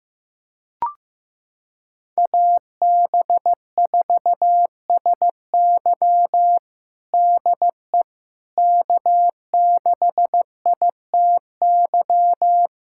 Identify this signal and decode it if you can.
Morse code, heard as a keyed tone, about 15 wpm, 700 Hz. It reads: AB4SY DE K6ITY
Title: USA Callsign DE Callsign - 5 Characters - 15wpm